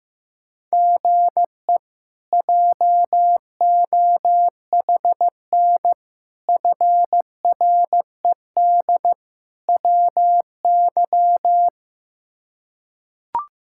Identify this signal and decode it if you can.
Morse code, heard as a keyed tone, about 15 wpm, 700 Hz. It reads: GE JOHN FRED WY
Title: K1USN SST QSO Practice - 15wpm